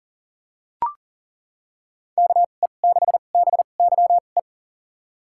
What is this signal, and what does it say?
Morse code, heard as a keyed tone, about 40 wpm, spaced as for 29 wpm, 700 Hz. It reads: KE6BYE
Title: USA Call Signs - 6 Characters - 2x Character Spacing - 40wpm